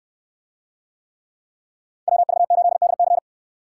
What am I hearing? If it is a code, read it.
Morse code, heard as a keyed tone, about 68 wpm, 700 Hz. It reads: WV8DC